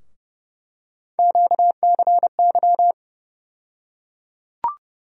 Morse code, heard as a keyed tone, about 30 wpm, 700 Hz. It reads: QCY
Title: USA Call Sign Postfixes - 3 Letters 30wpm